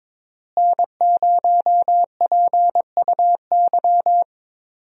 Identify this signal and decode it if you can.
Morse code, heard as a keyed tone, about 22 wpm, 700 Hz. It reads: N0PUY
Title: USA Call Signs - 5 Characters - Repeated 2x - 22wpm